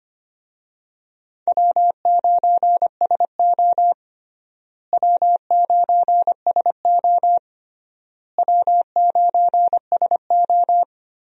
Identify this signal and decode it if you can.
Morse code, heard as a keyed tone, about 25 wpm, 700 Hz. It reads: W9SO W9SO W9SO
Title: USA Call Signs - 4 Characters - Repeated 3x - 25wpm